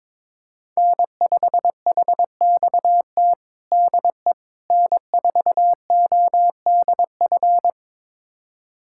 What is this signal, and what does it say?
Morse code, heard as a keyed tone, about 21 wpm, 700 Hz. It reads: N5HXT DE N4ODF
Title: USA Callsign DE Callsign - 5 Characters - 22wpm